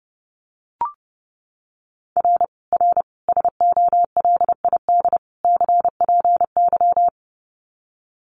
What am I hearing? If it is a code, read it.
Morse code, heard as a keyed tone, about 30 wpm, 700 Hz. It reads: R R SOLID CPY
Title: Common QSO Phrases 30wpm